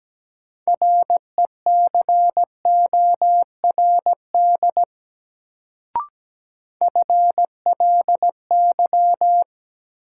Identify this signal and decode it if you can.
Morse code, heard as a keyed tone, about 17 wpm, 700 Hz. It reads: RECORD FLY
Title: Sentences from Top 500 Words - New Word Drill - 17wpm